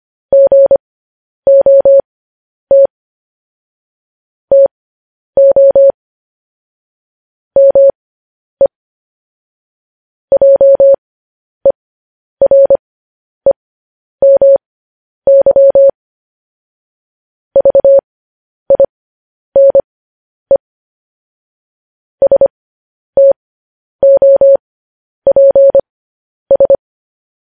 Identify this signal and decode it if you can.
Morse code, heard as a keyed tone, about 25 wpm, spaced as for 10 wpm, 550 Hz. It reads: GOT TO ME JEREMY VINE STOPS
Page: news headlines in morse code at 10 wpm